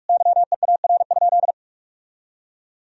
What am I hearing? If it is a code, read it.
Morse code, heard as a keyed tone, about 46 wpm, 700 Hz. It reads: YEAR?
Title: Sentences from Top 1000 Words - Review Part 3 - 45wpm